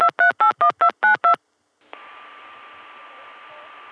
DTMF, heard as telephone touch tones, tones of about 110 ms. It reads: 2381292